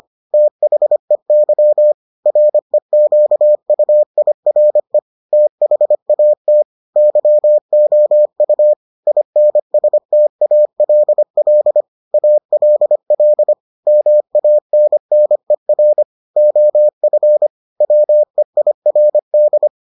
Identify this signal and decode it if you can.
Morse code, heard as a keyed tone, about 25 wpm, 600 Hz. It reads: THEY REQUIRE THAT YOU INSTALL ALL MANNER OF WEIRD